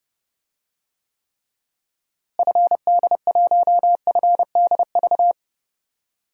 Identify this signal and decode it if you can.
Morse code, heard as a keyed tone, about 32 wpm, 700 Hz. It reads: FD1FDV
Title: DXCC-100 Call Signs - 6 Characters - Repeated 3x - 30wpm